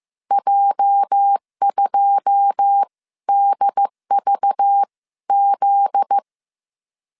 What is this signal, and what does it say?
Morse code, heard as a keyed tone, about 15 wpm, spaced as for 13 wpm, 800 Hz. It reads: J2DVZ
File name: CW / DET / 1 9 1500C S C 14wpm 3245